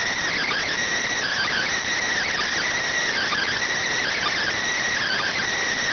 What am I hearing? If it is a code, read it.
Emergency Alert System (SAME) tones (header bursts), heard as harsh buzzing bursts. It is heard from 1.1 to 5.5 s.